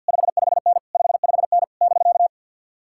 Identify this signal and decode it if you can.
Morse code, heard as a keyed tone, about 61 wpm, 700 Hz. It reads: 55N 55N <BK>
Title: Speed-Racing Common QSO Phrases - 50wpm